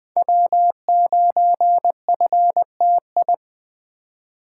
Morse code, heard as a keyed tone, about 20 wpm, 700 Hz. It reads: W9FTI
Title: USA Call Signs - 2x Character Spacing - 20wpm